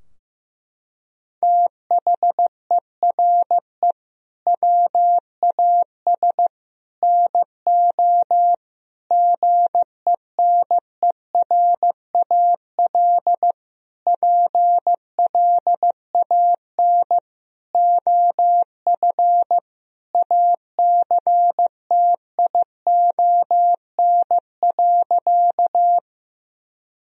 Morse code, heard as a keyed tone, about 15 wpm, 700 Hz. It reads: THERE WAS NO GENERAL PLAN OF ACTION.
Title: Sentences from Top 1000 Words - Review Part 4 - 15wpm